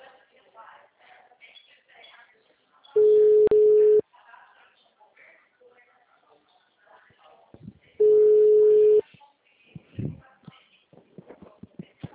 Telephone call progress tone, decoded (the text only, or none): ringback tone